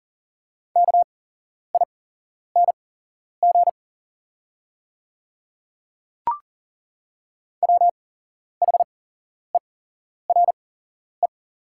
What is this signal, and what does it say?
Morse code, heard as a keyed tone, about 40 wpm, spaced as for 11 wpm, 700 Hz. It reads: KING WHERE
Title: 8x Character Spacing - Top 500 Words - 40wpm